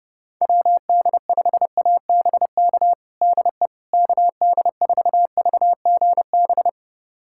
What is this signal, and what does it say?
Morse code, heard as a keyed tone, about 30 wpm, 700 Hz. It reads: WD5ABK DE KD4VGB
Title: USA Callsign DE Callsign 30wpm